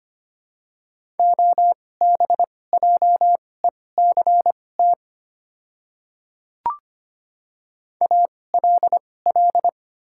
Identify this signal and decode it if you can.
Morse code, heard as a keyed tone, about 25 wpm, spaced as for 18 wpm, 700 Hz. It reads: OBJECT ALL
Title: 2x Character Spacing - Top 500 Words - 25wpm